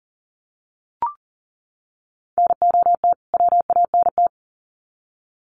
Morse code, heard as a keyed tone, about 40 wpm, 700 Hz. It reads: NOT WANT
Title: Sets of 2 Words 40wpm